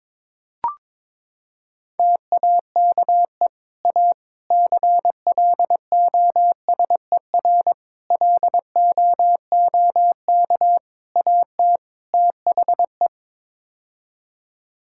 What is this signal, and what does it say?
Morse code, heard as a keyed tone, about 21 wpm, 700 Hz. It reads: TAKE A CLOSER LOOK AT THE
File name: practice / 22wpm / Sets of 6 Words - 22wpm